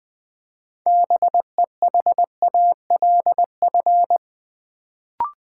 Morse code, heard as a keyed tone, about 20 wpm, 700 Hz. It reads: BEHALF
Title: Build A Word - 2 Letters At A Time - 6-Letter Words - 20wpm